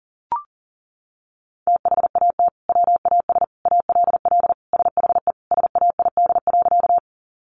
Morse code, heard as a keyed tone, about 40 wpm, 700 Hz. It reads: THAT WAS ALL SHE SAID.
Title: Sentences from Top 100 Words 40wpm